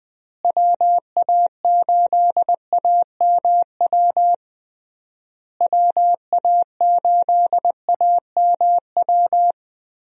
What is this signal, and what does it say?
Morse code, heard as a keyed tone, about 20 wpm, 700 Hz. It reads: WA8AMW WA8AMW
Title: USA Call Signs - 6 Characters - Repeated 3x - 20wpm